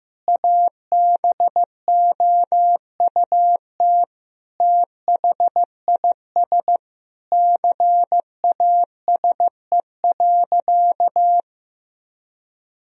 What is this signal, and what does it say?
Morse code, heard as a keyed tone, about 15 wpm, 700 Hz. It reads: ABOUT THIS CASE.
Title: Sentences from Top 1000 Words - Review Part 4 - 15wpm